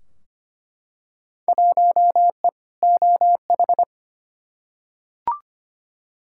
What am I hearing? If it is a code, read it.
Morse code, heard as a keyed tone, about 25 wpm, 700 Hz. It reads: 1E OH
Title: Field Day Exchanges - 25wpm